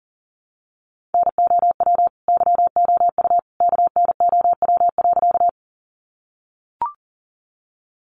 Morse code, heard as a keyed tone, about 40 wpm, 700 Hz. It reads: NOW YOU KNOW.
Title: Sentences from Top 100 Words 40wpm